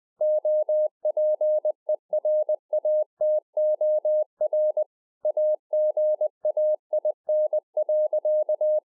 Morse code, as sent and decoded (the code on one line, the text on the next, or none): --- .--. .-. .- - --- .-. / .- --. .- .. -. .-.-.-
OPERATOR AGAIN.